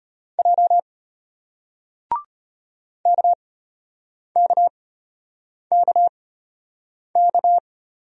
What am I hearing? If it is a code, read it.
Morse code, heard as a keyed tone, about 34 wpm, spaced as for 8 wpm, 700 Hz. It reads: J KKKK